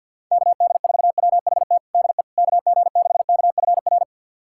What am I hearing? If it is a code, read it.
Morse code, heard as a keyed tone, about 51 wpm, 700 Hz. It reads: KD4WHT DE KG6KFR
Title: USA Callsign DE Callsign - 6 Characters - 50wpm